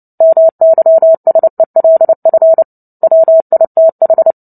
Morse code, heard as a keyed tone, about 30 wpm, 650 Hz. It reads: MYSELF WITH
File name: Book Two / 03-Chapter 3 - 29 WPM